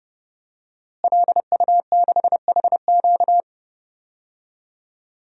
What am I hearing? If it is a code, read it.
Morse code, heard as a keyed tone, about 30 wpm, 700 Hz. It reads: LU6HQ
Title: DXCC-100 Call Signs - 5 Characters - Repeated 2x - 30wpm